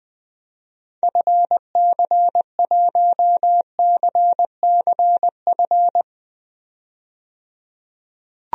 Morse code, heard as a keyed tone, about 20 wpm, 700 Hz. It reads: FC1CCF
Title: DXCC-100 Call Signs - 6 Characters - 20wpm